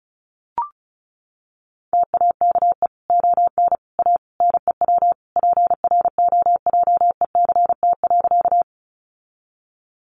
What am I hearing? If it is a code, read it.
Morse code, heard as a keyed tone, about 34 wpm, 700 Hz. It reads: TAKE ON A NEW PROJECT.
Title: Sentences from Top 1400 Words 35wpm